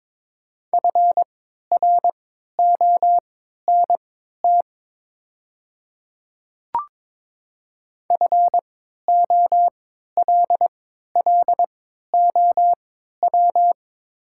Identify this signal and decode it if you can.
Morse code, heard as a keyed tone, about 22 wpm, spaced as for 12 wpm, 700 Hz. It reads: FRONT FOLLOW
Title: 3x Character Spacing - Top 500 Words - 22wpm